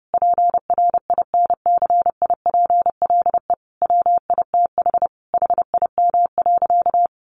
Morse code, heard as a keyed tone, about 30 wpm, 700 Hz. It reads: PRINCIPLE WITH HIM.
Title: Sentences from Top 1400 Words 30wpm